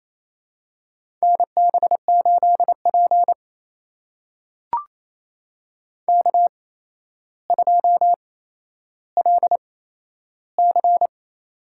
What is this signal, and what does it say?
Morse code, heard as a keyed tone, about 28 wpm, spaced as for 12 wpm, 700 Hz. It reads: NB8P K2LC
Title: USA Call Signs - 4 Characters - 8x Character Spacing - 28wpm